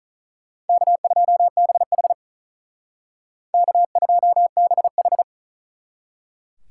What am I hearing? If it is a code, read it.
Morse code, heard as a keyed tone, about 44 wpm, 700 Hz. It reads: K2BH K2BH